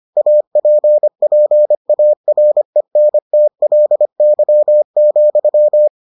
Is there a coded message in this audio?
Yes: Morse code from 0.2 to 5.9 s, heard as a keyed tone, about 25 wpm, 600 Hz. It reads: APPARENTLY,